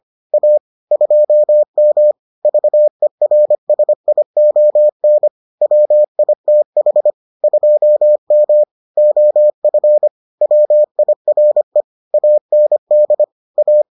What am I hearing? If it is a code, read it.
Morse code, heard as a keyed tone, about 25 wpm, 600 Hz. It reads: A 2M VERSION WITH 2M OF WIRE AND A